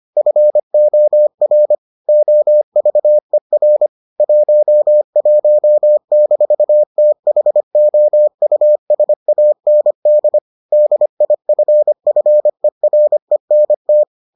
Morse code, heard as a keyed tone, about 25 wpm, 600 Hz. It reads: FOR OVER 11-THOUSAND DIFFERENT